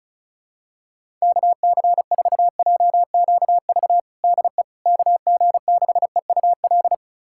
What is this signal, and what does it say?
Morse code, heard as a keyed tone, about 36 wpm, 700 Hz. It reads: KC4JQV DE KG6EUL
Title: USA Callsign DE Callsign - 6 Characters - 35wpm